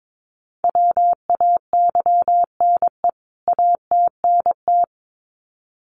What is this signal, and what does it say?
Morse code, heard as a keyed tone, about 22 wpm, 700 Hz. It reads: WAYNE ATNT
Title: CWT Exchange 22wpm